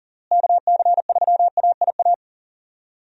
Morse code, heard as a keyed tone, about 40 wpm, 700 Hz. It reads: KC3AIA
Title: USA Call Signs - 6 Characters - Repeated 3x - 40wpm